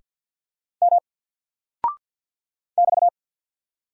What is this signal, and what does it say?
Morse code, heard as a keyed tone, about 49 wpm, 700 Hz. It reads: M <BT>
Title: Single Character (Letter:Number:Punctuation:Prosign) Alt 50wpm